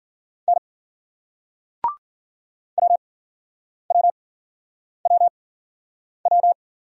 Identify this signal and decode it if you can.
Morse code, heard as a keyed tone, about 56 wpm, spaced as for 9 wpm, 700 Hz. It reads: N WWWW